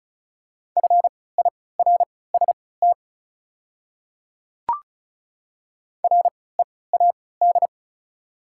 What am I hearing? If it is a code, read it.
Morse code, heard as a keyed tone, about 36 wpm, spaced as for 20 wpm, 690 Hz. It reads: FIRST READ
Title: 3x Character Spacing - Top 500 Words - 35wpm